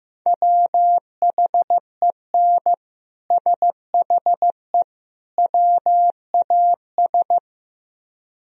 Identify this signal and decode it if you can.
Morse code, heard as a keyed tone, about 16 wpm, 700 Hz. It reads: WHEN SHE WAS
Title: Speed-Racing - Sets of 3 Words 15wpm